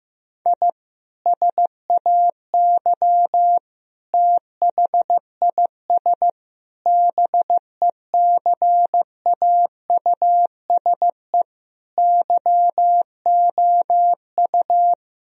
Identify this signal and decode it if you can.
Morse code, heard as a keyed tone, about 15 wpm, 700 Hz. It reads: I SAY THIS BECAUSE YOU
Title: Sentences from Top 1000 Words - Review Part 3 - 15wpm